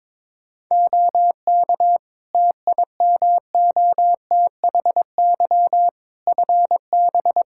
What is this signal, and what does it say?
Morse code, heard as a keyed tone, about 22 wpm, 700 Hz. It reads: OK TIMOTHY FB
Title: Practice QSO - Exchange 2 - Info - 22wpm